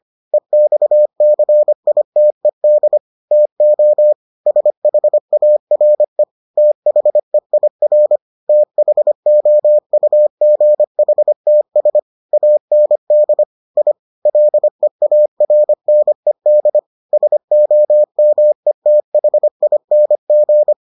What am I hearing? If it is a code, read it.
Morse code, heard as a keyed tone, about 25 wpm, 600 Hz. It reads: EXCITED TO SHARE THEIR THOUGHTS AND I LEARNED SOMETHING